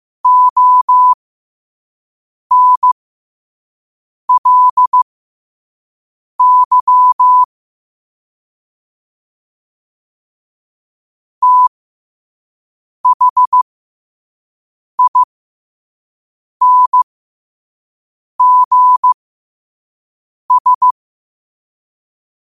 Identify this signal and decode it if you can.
Morse code, heard as a keyed tone, about 15 wpm, spaced as for 5 wpm, 1000 Hz. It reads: ONLY THINGS
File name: qotdfiles / 5wpm